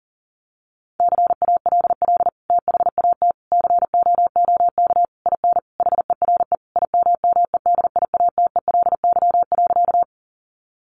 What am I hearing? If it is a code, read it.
Morse code, heard as a keyed tone, about 40 wpm, 700 Hz. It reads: CALL THAT COOK IN HERE IMMEDIATELY.
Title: Sentences from Top 1100 Words 40wpm